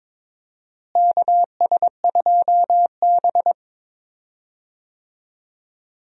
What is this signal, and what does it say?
Morse code, heard as a keyed tone, about 22 wpm, 700 Hz. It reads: KS2B
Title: USA Call Signs - 4 Characters - Repeated 3x - 22wpm